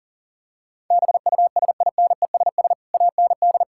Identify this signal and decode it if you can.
Morse code, heard as a keyed tone, about 40 wpm, 700 Hz. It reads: BUSINESS AND
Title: Sentences from Top 1000 Words - Review Part 5 - 40wpm